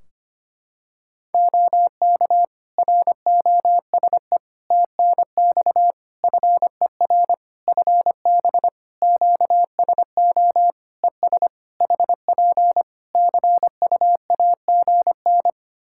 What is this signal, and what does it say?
Morse code, heard as a keyed tone, about 25 wpm, 700 Hz. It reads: OK ROSE TNX FER FB QSO ES HP CUAGN
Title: Practice QSO - Exchange 4 - Ending - 25wpm